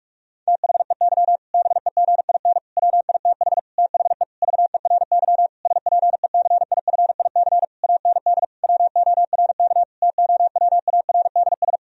Morse code, heard as a keyed tone, about 46 wpm, 700 Hz. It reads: THEY BEGIN WITH THE VERY SPECIFIC AND WORK TOWARDS
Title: Sentences from Top 1000 Words - Review Part 3 - 45wpm